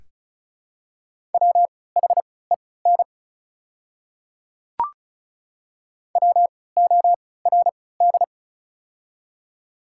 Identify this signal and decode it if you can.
Morse code, heard as a keyed tone, about 34 wpm, spaced as for 20 wpm, 690 Hz. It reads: WHEN WORD
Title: 3x Character Spacing - Top 100 Words - 35wpm